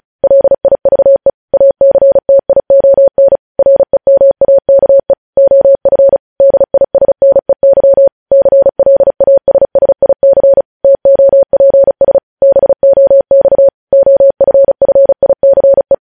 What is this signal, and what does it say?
Morse code, heard as a keyed tone, about 36 wpm, 550 Hz. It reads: LIVE ACTION REMAKE OF DISNEY CLASSIC TOPS BOX OFFICE
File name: podcast / News headlines-1x-35x35